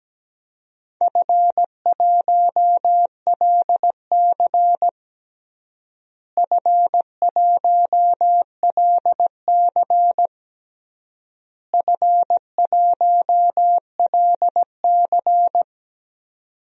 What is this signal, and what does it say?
Morse code, heard as a keyed tone, about 17 wpm, 700 Hz. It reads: F1LC F1LC F1LC
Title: DXCC-100 Call Signs - 4 Characters - Repeated 3x - 17wpm